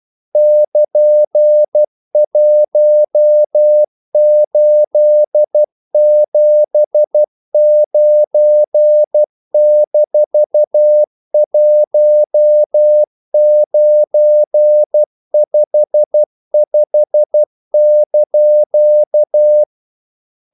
Morse code, heard as a keyed tone, about 12 wpm, 600 Hz. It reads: <KN>1879-1955)
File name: CW / Practice / EnglishQuotations0112